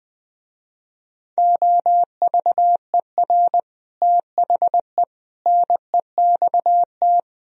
Morse code, heard as a keyed tone, about 20 wpm, 700 Hz. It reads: OVER THE NEXT
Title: Speed-Racing - Sets of 5 Words 20wpm